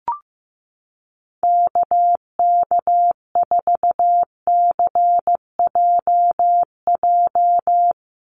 Morse code, heard as a keyed tone, about 15 wpm, 700 Hz. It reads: KK4CJJ
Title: USA Call Signs 15wpm